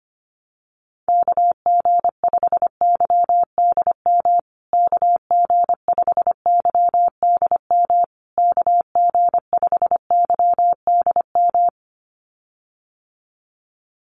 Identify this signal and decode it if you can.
Morse code, heard as a keyed tone, about 25 wpm, 700 Hz. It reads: KG5YDM KG5YDM KG5YDM
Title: USA Call Signs Repeated 3x 25wpm